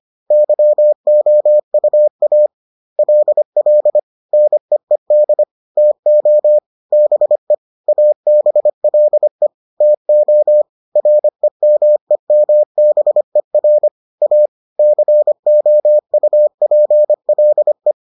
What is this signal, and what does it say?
Morse code, heard as a keyed tone, about 25 wpm, 600 Hz. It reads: YOUA LL NEED TO BE ABLE TO REMEMBER A COUPLE